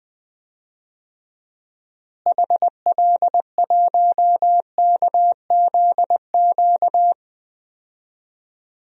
Morse code, heard as a keyed tone, about 20 wpm, 700 Hz. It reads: HL1KZQ
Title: DXCC-100 Call Signs - 6 Characters - Repeated 2x - 20wpm